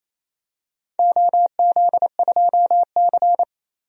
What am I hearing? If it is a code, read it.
Morse code, heard as a keyed tone, about 28 wpm, 700 Hz. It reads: OZ2C
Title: DXCC-100 Call Signs - 4 Characters - 28wpm